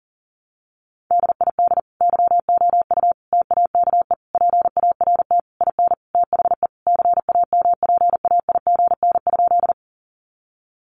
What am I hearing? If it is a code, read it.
Morse code, heard as a keyed tone, about 40 wpm, 700 Hz. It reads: DID YOU TAKE PART IN THE CAMPAIGN?
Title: Sentences from Top 1200 Words 40wpm